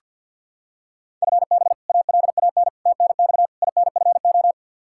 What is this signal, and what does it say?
Morse code, heard as a keyed tone, about 49 wpm, 700 Hz. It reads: FB ALAN TNX INFO
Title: Practice QSO - Exchange 3 - More Info - 50wpm